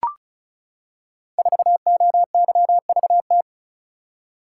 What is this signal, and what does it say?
Morse code, heard as a keyed tone, about 36 wpm, 690 Hz. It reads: 4OYVT
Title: Code Groups - 5 Letter:Numbers - 35wpm